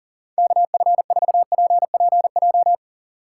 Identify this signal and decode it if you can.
Morse code, heard as a keyed tone, about 40 wpm, 700 Hz. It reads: KF4PPJ
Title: USA Call Signs - 4x Character Spacing - 40wpm